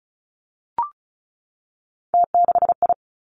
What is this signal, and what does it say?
Morse code, heard as a keyed tone, about 36 wpm, 700 Hz. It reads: T6I
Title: Three Challenging Letter-Number Combinations 35wpm